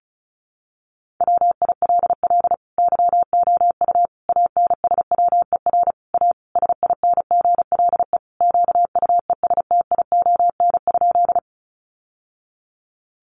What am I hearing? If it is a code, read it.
Morse code, heard as a keyed tone, about 34 wpm, 700 Hz. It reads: WILL YOU ANSWER A SINGLE QUESTION?
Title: Sentences from Top 700 Words 35wpm